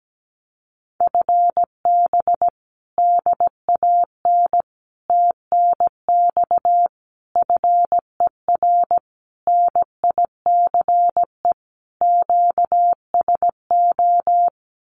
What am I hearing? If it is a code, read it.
Morse code, heard as a keyed tone, about 17 wpm, 700 Hz. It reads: FB DAN TNX FER NICE QSO
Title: Common QSO Phrases Encore 17wpm